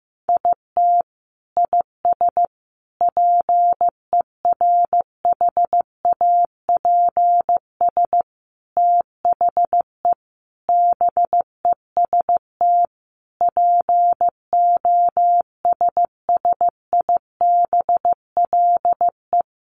Morse code, heard as a keyed tone, about 15 wpm, 700 Hz. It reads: IT IS PERHAPS THE BEST POSSIBLE
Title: Sentences from Top 1300 Words 15wpm